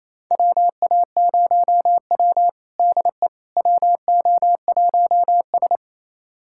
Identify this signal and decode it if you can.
Morse code, heard as a keyed tone, about 28 wpm, 700 Hz. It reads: WA0W DE WO1S
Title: USA Callsign DE Callsign - 4 Characters - 28wpm